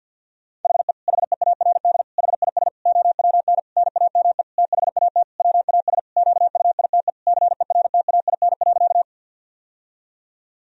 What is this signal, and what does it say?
Morse code, heard as a keyed tone, about 47 wpm, 700 Hz. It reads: HE HEARD HIS OWN NAME THAT WAS QUITE CERTAIN.
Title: Sentences from Top 1000 Words - Review Part 2 - 50wpm